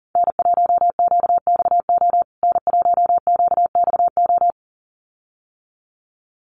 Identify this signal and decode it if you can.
Morse code, heard as a keyed tone, about 40 wpm, 700 Hz. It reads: N1QXO N1QXO
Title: USA Call Signs Repeated 2x 40wpm